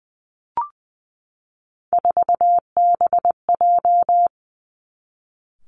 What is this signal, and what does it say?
Morse code, heard as a keyed tone, about 20 wpm, 700 Hz. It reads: 4BJ
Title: Three Challenging Letter-Number Combinations 20wpm